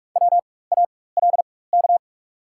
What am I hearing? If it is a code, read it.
Morse code, heard as a keyed tone, about 45 wpm, spaced as for 21 wpm, 690 Hz. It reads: WALK
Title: 4x Character Spacing - Top 300 Words - 45wpm